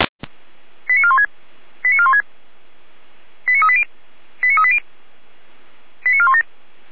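Five-tone selective call (ZVEI), heard as quick runs of five tones, tones of about 70 ms. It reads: 89416 89416 89380 89380 89417
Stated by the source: ZVEI five-tone call received by SDR, fire brigade feuerwehr-orlamuende (F85360).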